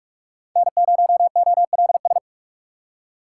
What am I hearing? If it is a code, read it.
Morse code, heard as a keyed tone, about 45 wpm, 700 Hz. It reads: N0OLS
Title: USA Call Signs - 5 Characters - 3x Character Spacing - 45wpm